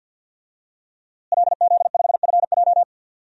Morse code, heard as a keyed tone, about 52 wpm, 700 Hz. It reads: LZ5FJ
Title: DXCC-100 Call Signs - 5 Characters - Repeated 2x - 50wpm